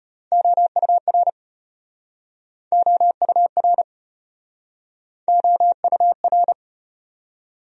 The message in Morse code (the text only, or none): OUR OUR OUR